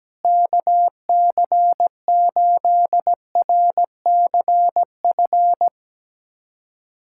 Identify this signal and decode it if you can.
Morse code, heard as a keyed tone, about 17 wpm, 700 Hz. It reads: KC8RCF
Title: USA Callsign DE Callsign - 6 Characters - 17wpm